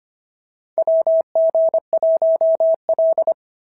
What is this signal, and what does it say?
Morse code, heard as a keyed tone, about 25 wpm, 690 Hz. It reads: WG1L
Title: SOTA QSO Practice 25wpm